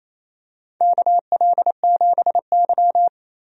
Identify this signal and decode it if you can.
Morse code, heard as a keyed tone, about 27 wpm, 700 Hz. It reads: KL7Y
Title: DXCC-100 Call Signs - 4 Characters - 28wpm